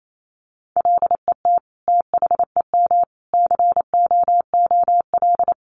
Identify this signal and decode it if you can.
Morse code, heard as a keyed tone, about 28 wpm, 700 Hz. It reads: LET THEM COOL
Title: Sentences from Top 1200 Words 28wpm